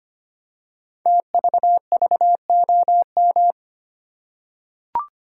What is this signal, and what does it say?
Morse code, heard as a keyed tone, about 25 wpm, 700 Hz. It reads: TVVOM
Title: Code Groups - 5 Letters - 25wpm